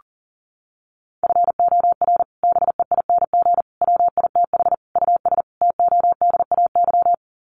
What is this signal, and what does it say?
Morse code, heard as a keyed tone, about 37 wpm, 700 Hz. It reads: FOR BEING WITH US TODAY
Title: Sets of 5 Words Encore2 40wpm